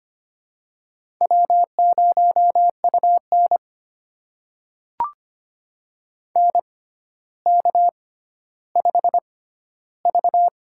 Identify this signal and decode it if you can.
Morse code, heard as a keyed tone, about 25 wpm, spaced as for 13 wpm, 700 Hz. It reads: W0UN NK5V
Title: USA Call Signs - 4 Characters - 6x Character Spacing - 25wpm